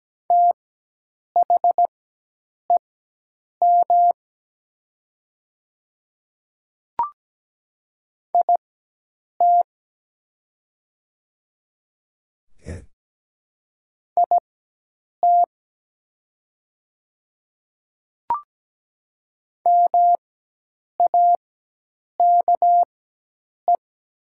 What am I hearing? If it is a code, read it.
Morse code, heard as a keyed tone, about 17 wpm, spaced as for 8 wpm, 700 Hz. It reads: THEM IT IT MAKE
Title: 4x Character Spacing - Top 100 Words - 17wpm